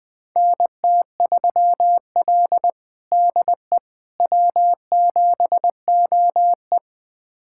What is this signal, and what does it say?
Morse code, heard as a keyed tone, about 20 wpm, 700 Hz. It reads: NT3L DE W7OE
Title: USA Callsign DE Callsign - 4 Characters - 20wpm